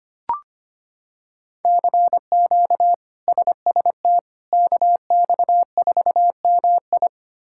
Morse code, heard as a keyed tone, about 25 wpm, 700 Hz. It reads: CQ SST KX4MI
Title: K1USN SST QSO Practice - 25wpm